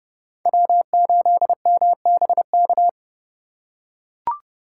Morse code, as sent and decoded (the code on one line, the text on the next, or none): .-- ---.. -- -... -.-
W8MBK